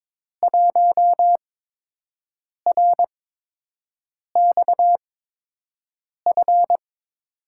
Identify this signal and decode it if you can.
Morse code, heard as a keyed tone, about 22 wpm, spaced as for 6 wpm, 700 Hz. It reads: 1RXF
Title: USA Call Signs - 6 Characters - 8x Character Spacing - 22wpm